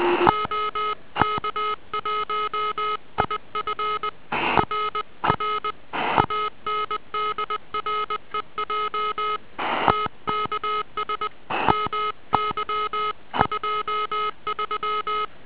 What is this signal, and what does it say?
Morse code, heard as a keyed tone, about 20 wpm, 1270 Hz. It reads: OK1IF RR ANDREJ TKS MY23